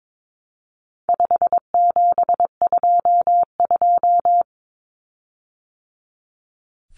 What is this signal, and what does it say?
Morse code, heard as a keyed tone, about 22 wpm, 700 Hz. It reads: 5722
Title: Common Cut Numbers - 4-Digit 22wpm